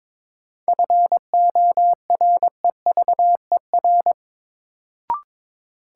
Morse code, heard as a keyed tone, about 23 wpm, 700 Hz. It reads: FOREVER
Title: Sentences from Top 1600 Words - New Word Drill - 22wpm